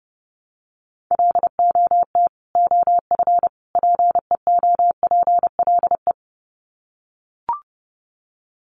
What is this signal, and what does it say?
Morse code, heard as a keyed tone, about 30 wpm, 700 Hz. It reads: LOT OF PEOPLE
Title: Sets of 3 Words 30wpm